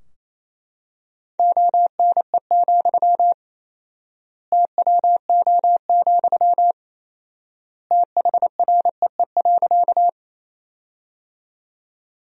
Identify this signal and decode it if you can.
Morse code, heard as a keyed tone, about 28 wpm, 700 Hz. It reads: ONE, TWO, THREE.
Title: 4x Word-Spacing - Sentences from Top 200 Words - 28wpm